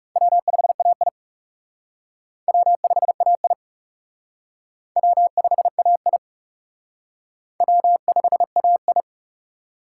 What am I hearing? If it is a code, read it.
Morse code, heard as a keyed tone, about 36 wpm, 700 Hz. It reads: W5AI W5AI W5AI W5AI
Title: Speed Racing - USA 1x2 Call Signs 30wpm